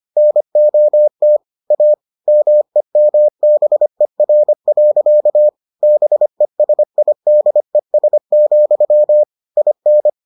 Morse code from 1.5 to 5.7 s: A MEMBER.

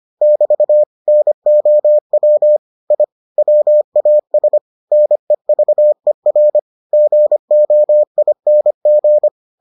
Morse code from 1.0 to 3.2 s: NOW I

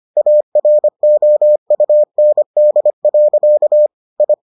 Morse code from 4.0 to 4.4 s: I

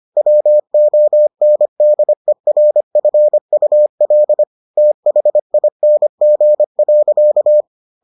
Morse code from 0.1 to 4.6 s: WONDERFUL

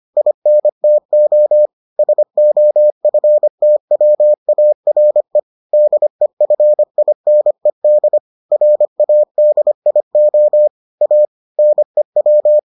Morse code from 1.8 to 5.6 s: SOFTWARE

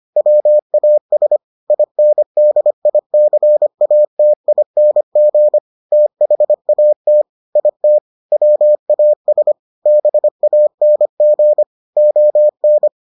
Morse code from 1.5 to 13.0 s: INDICATING THAT IT WAS BANG ON

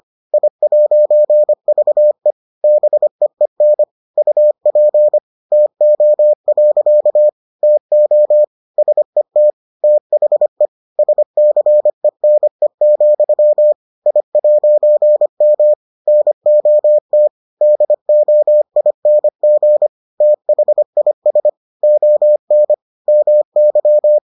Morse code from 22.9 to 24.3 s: MY

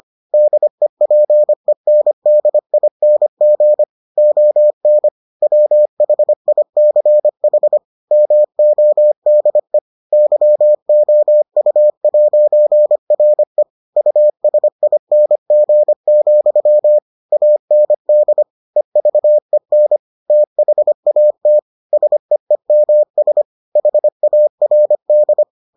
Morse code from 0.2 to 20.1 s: DEPENDING ON WHICH MODE YOU'RE USING, AND EVEN

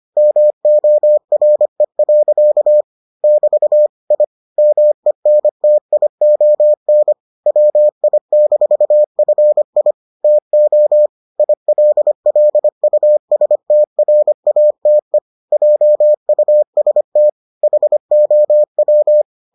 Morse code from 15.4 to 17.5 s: JUST

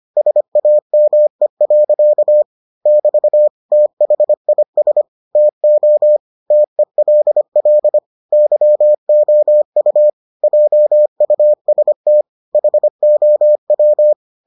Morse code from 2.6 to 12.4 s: <BT> THIS TO TELL YOU JUST

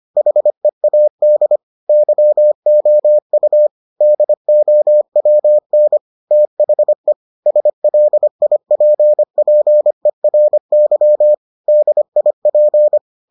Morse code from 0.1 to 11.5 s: HEAD YOU DOWN THE SLIPPERY

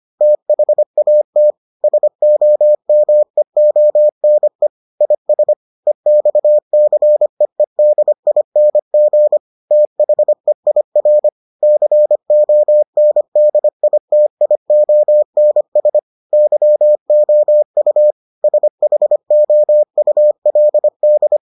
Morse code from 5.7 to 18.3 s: EXCEEDING THEIR CONDITIONS YOU

